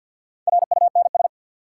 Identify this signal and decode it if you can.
Morse code, heard as a keyed tone, about 52 wpm, 700 Hz. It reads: RUNS
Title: Sentences from Top 600 Words - New Word Drill - 50wpm